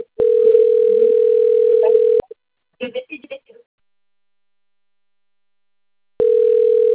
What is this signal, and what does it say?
Ringback tone (North America), 440 + 480 Hz, with speech.
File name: rg-1600-unknown- - - .35084